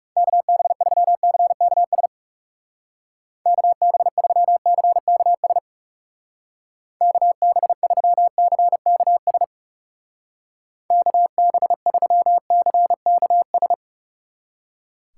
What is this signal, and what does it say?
Morse code, heard as a keyed tone, about 39 wpm, 700 Hz. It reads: KB3CKS KB3CKS KB3CKS KB3CKS